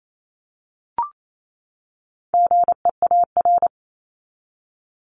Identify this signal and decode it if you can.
Morse code, heard as a keyed tone, about 28 wpm, 700 Hz. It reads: GEAR
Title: Top 1251-1500 Words 28wpm